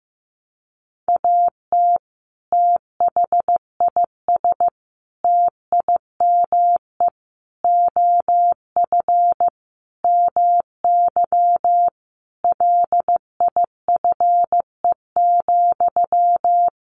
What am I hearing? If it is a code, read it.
Morse code, heard as a keyed tone, about 15 wpm, 700 Hz. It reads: AT THIS TIME OF MY LIFE,
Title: Sentences from Top 300 Words 15wpm